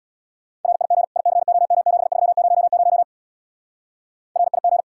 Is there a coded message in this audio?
Yes: Morse code from 0.7 to 4.8 s, heard as a keyed tone, about 66 wpm, 700 Hz. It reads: RIG ICOM7300 RIG